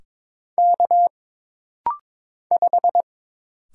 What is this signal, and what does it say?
Morse code, heard as a keyed tone, about 22 wpm, 700 Hz. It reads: K 5
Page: This rapid-fire timing is geared to push your ICR (Instant Character Recognition) skills.